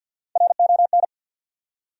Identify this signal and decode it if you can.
Morse code, heard as a keyed tone, about 47 wpm, 700 Hz. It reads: RON
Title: USA Common Names - Max 3 Letters - 50wpm